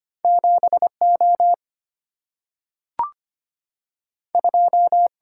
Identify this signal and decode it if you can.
Morse code, heard as a keyed tone, about 25 wpm, 700 Hz. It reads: 7 O 2
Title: Single-Letter Single-Number 25wpm